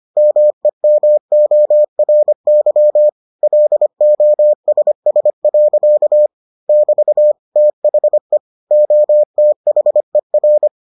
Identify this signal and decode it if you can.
Morse code, heard as a keyed tone, about 25 wpm, 600 Hz. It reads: MEMORY LOSS. <BT> THE OTHER